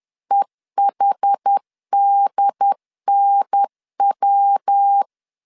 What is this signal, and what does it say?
Morse code, heard as a keyed tone, about 10 wpm, 800 Hz. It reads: EHDNW
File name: CW / DET / 1 3 1500C S B 10wpm 3335